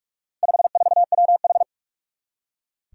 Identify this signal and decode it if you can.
Morse code, heard as a keyed tone, about 46 wpm, 700 Hz. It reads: 54WH
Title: Four Challenging Letter-Number Combinations 45wpm